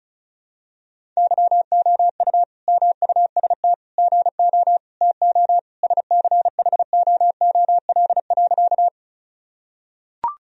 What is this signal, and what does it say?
Morse code, heard as a keyed tone, about 36 wpm, 700 Hz. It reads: YOU MUST GO TO SCHOOL.
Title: Sentences from Top 1000 Words - Review Part 1 - 35wpm